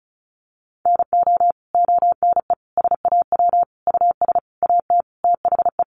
Morse code, heard as a keyed tone, about 34 wpm, 700 Hz. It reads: NO ONE SAW US AT THE
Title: Sentences from Top 1200 Words 35wpm